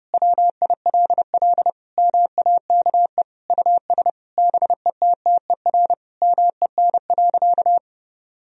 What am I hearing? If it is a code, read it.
Morse code, heard as a keyed tone, about 30 wpm, 700 Hz. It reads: WILL MAKE US BETTER MEN.
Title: Sentences from Top 1000 Words - Review Part 3 - 30wpm